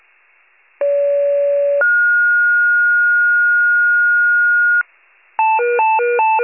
Two-tone page, heard as a one-second tone, then a three-second tone, about 570 then 1430 Hz. It is heard from 0.8 to 4.8 s.
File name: wcalert